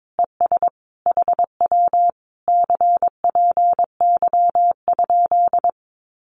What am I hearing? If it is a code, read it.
Morse code, heard as a keyed tone, about 22 wpm, 700 Hz. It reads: ES HW CPY?
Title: Common QSO Phrases Encore 22wpm